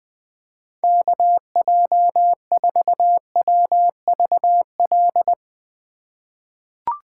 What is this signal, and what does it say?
Morse code, heard as a keyed tone, about 19 wpm, 700 Hz. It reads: KJ4WVL
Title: USA Call Signs - 2x Character Spacing - 20wpm